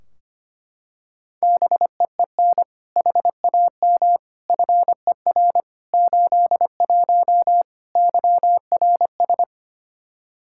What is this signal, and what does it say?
Morse code, heard as a keyed tone, about 25 wpm, 700 Hz. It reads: BEEN HAM FER 81 YRS
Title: Practice QSO - Exchange 3 - More Info - 25wpm